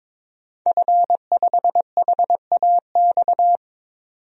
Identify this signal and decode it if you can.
Morse code, heard as a keyed tone, about 22 wpm, 700 Hz. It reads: F5HAX
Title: DXCC-100 Call Signs - 5 Characters - 22wpm